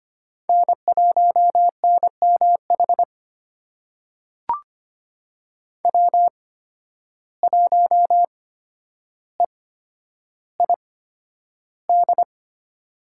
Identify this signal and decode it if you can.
Morse code, heard as a keyed tone, about 25 wpm, spaced as for 11 wpm, 700 Hz. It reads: N1NMH W1EID
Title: USA Call Signs - 5 Characters - 8x Character Spacing - 25wpm